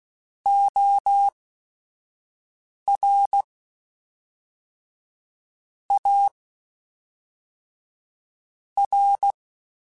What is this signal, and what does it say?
Morse code, heard as a keyed tone, about 16 wpm, 800 Hz. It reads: O R A R
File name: Technical Subjects / Technical Subjects-mp3s / Sw Pwr Sup @5